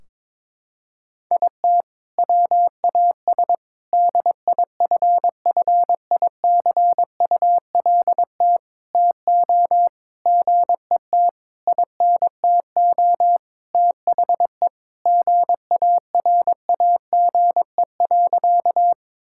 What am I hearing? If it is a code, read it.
Morse code, heard as a keyed tone, about 22 wpm, 700 Hz. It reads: IT WAS DIFFICULT TO GET INTO THE GARAGE.
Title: Sentences from Top 1600 Words 22wpm